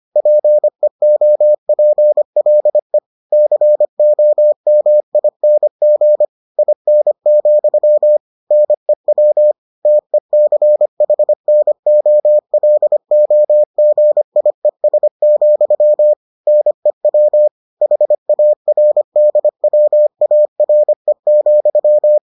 Morse code, heard as a keyed tone, about 25 wpm, 600 Hz. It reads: PEOPLE COMING IN, NEW TECHNOLOGIES, NEW HARDWARE,